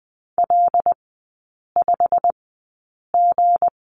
Morse code, heard as a keyed tone, about 20 wpm, spaced as for 8 wpm, 700 Hz. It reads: L5G
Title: 166 Lesson 27e - Number 7 - Three Characters 20wpm